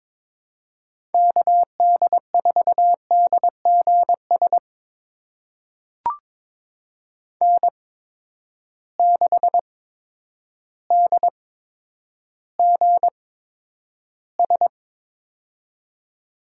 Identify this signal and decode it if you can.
Morse code, heard as a keyed tone, about 22 wpm, spaced as for 10 wpm, 700 Hz. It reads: KD4DGS N6DGS
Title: USA Call Signs - 8x Character Spacing - 22wpm